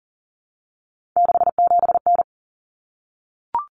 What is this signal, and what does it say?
Morse code, heard as a keyed tone, about 40 wpm, 700 Hz. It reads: 67N
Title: Common Cut Numbers - 3-Digit 40wpm